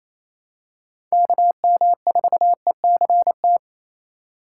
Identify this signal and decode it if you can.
Morse code, heard as a keyed tone, about 28 wpm, 700 Hz. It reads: KM4ECT
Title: USA Call Signs - 6 Characters - 2x Character Spacing - 28wpm